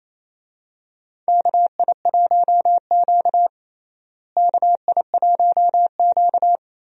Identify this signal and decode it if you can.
Morse code, heard as a keyed tone, about 28 wpm, 700 Hz. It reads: KI1Q KI1Q
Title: USA Call Signs - 4 Characters - Repeated 2x - 28wpm